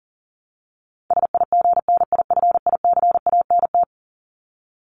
Morse code, heard as a keyed tone, about 40 wpm, 700 Hz. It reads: SIGNIFICANT
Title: Top 751-1000 Words 40wpm